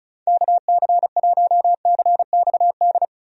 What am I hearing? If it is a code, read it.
Morse code, heard as a keyed tone, about 36 wpm, 700 Hz. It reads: KC1CXD
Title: USA Call Signs - 6 Characters - 35wpm